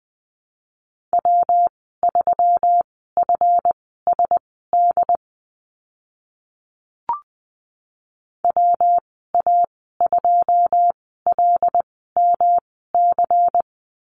Morse code, heard as a keyed tone, about 20 wpm, spaced as for 14 wpm, 700 Hz. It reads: W3FSD WA2LMC
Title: 147 Lesson 24g - Letter y - Call Signs 20wpm